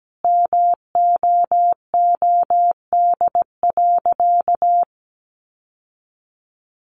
Morse code, heard as a keyed tone, about 17 wpm, 700 Hz. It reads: MOOD.
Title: Sentences from Top 1100 Words 17wpm